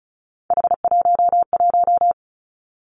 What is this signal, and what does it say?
Morse code, heard as a keyed tone, about 36 wpm, 700 Hz. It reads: H11
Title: Three Challenging Letter-Number Combinations 35wpm